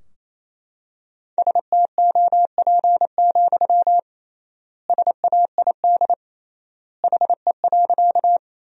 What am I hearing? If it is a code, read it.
Morse code, heard as a keyed tone, about 28 wpm, 700 Hz. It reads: STOP, SAID HE.
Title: 3x Word-Spacing - Sentences from Top 300 Words - 28wpm